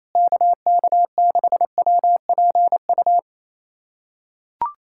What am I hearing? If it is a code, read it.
Morse code, heard as a keyed tone, about 28 wpm, 700 Hz. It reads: KK6WPU
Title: USA Callsign DE Callsign 28wpm